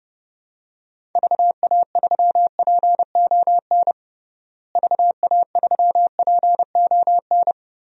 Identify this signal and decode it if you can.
Morse code, heard as a keyed tone, about 30 wpm, 700 Hz. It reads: VA3PON VA3PON
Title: DXCC-100 Call Signs - 6 Characters - Repeated 3x - 30wpm